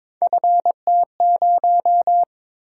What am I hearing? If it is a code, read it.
Morse code, heard as a keyed tone, about 22 wpm, 700 Hz. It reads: FT0
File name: Three Letter-Number Combinations - 22wpm